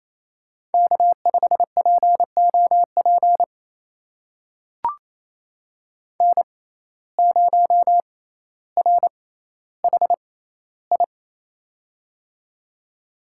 Morse code, heard as a keyed tone, about 27 wpm, spaced as for 14 wpm, 700 Hz. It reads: K5POP N0RHI
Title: USA Call Signs - 5 Characters - 6x Character Spacing - 28wpm